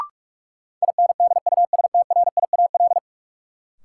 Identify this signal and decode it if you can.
Morse code, heard as a keyed tone, about 46 wpm, 700 Hz. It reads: INDUSTRIAL
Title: Sets of 5 Words - New Word Drill - 45wpm